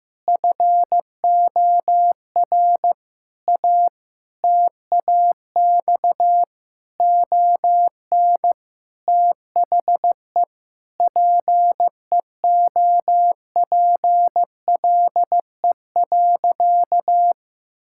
Morse code, heard as a keyed tone, about 15 wpm, 700 Hz. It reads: FOR A TAX ON THE PEOPLE.
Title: Sentences from Top 1000 Words - Review Part 5 - 15wpm